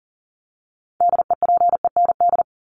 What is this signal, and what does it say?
Morse code, heard as a keyed tone, about 40 wpm, 700 Hz. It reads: DEPEND
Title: ING Suffix - Top 100 Words with Base Words 40wpm